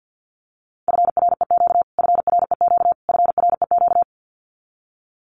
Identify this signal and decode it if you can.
Morse code, heard as a keyed tone, about 50 wpm, 700 Hz. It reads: FREQ FREQ FREQ
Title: Abbreviations Repeated 3x - 50wpm